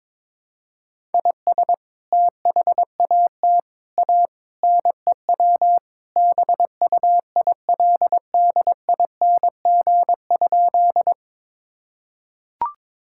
Morse code, heard as a keyed tone, about 22 wpm, 700 Hz. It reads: IS THAT A NEW BUILDING?
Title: Sentences from Top 1000 Words - Review Part 5 - 22wpm